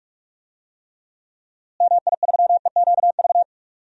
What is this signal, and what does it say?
Morse code, heard as a keyed tone, about 46 wpm, 700 Hz. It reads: MI3EQV
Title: DXCC-100 Call Signs - 6 Characters - Repeated 3x - 45wpm